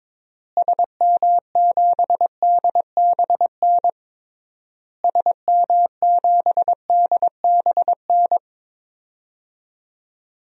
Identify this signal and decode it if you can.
Morse code, heard as a keyed tone, about 22 wpm, 700 Hz. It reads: SM7DBN SM7DBN
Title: DXCC-100 Call Signs - 6 Characters - Repeated 3x - 22wpm